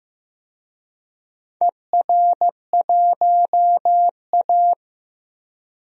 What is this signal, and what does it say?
Morse code, heard as a keyed tone, about 15 wpm, 700 Hz. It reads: ER1A
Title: DXCC-100 Call Signs - 4 Characters - Repeated 3x - 15wpm